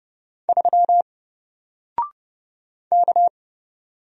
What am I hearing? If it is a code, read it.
Morse code, heard as a keyed tone, about 30 wpm, 700 Hz. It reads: 3 K
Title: Single Character (Letter:Number:Punctuation:Prosign) Alt 30wpm